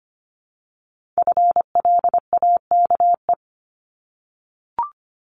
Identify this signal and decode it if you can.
Morse code, heard as a keyed tone, about 25 wpm, 700 Hz. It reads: FLAKE
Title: Multi-letter Phonemes Repeated 3x 25wpm